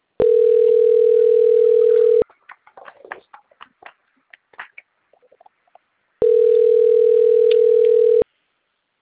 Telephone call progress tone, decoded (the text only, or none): ringback tone